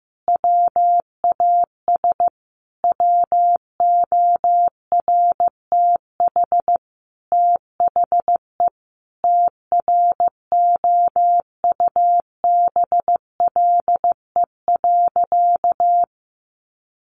Morse code, heard as a keyed tone, about 15 wpm, 700 Hz. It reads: WAS WORTH THE TROUBLE.
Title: Sentences from Top 1500 Words 15wpm